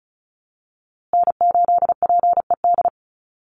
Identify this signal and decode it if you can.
Morse code, heard as a keyed tone, about 34 wpm, 700 Hz. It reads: N8PED
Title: USA Call Signs Repeated 2x 35wpm